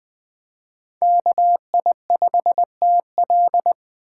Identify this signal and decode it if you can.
Morse code, heard as a keyed tone, about 20 wpm, 700 Hz. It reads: KI5TL
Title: Speed Racing - USA 2x2 Call Signs 17wpm